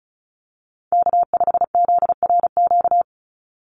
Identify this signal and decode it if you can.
Morse code, heard as a keyed tone, about 34 wpm, 700 Hz. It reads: K5ZRQ
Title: USA Call Signs 35wpm